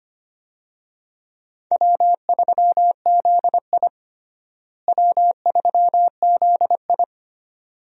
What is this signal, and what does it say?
Morse code, heard as a keyed tone, about 25 wpm, 700 Hz. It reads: W3ZI W3ZI
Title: USA Call Signs - 4 Characters - Repeated 2x - 25wpm